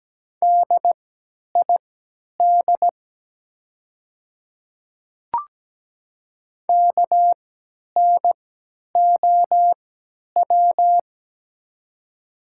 Morse code, heard as a keyed tone, about 17 wpm, spaced as for 10 wpm, 700 Hz. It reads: DID KNOW
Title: 3x Character Spacing - Top 100 Words - 17wpm